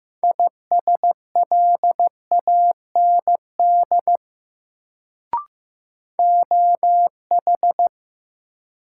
Morse code, heard as a keyed tone, about 15 wpm, 700 Hz. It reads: ISLAND OH
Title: Sentences from Top 500 Words - New Word Drill - 15wpm